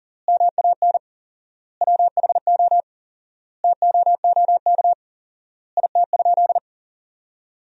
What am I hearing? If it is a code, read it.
Morse code, heard as a keyed tone, about 40 wpm, 700 Hz. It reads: MAN WHO TOOK IT?
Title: 4x Word-Spacing - Sentences from Top 300 Words - 40wpm